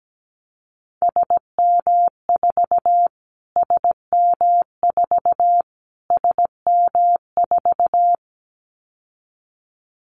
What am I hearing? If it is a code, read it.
Morse code, heard as a keyed tone, about 17 wpm, 700 Hz. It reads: SM4 SM4 SM4
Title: Callsign Prefixes for Top 25 DXCC - Repeated 3x - 17wpm